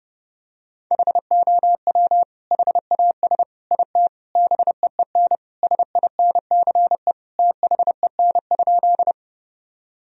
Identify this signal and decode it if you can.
Morse code, heard as a keyed tone, about 30 wpm, 700 Hz. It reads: HOW HAS IT BEEN SINCE THEN?
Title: Sentences from Top 1000 Words - Review Part 1 - 30wpm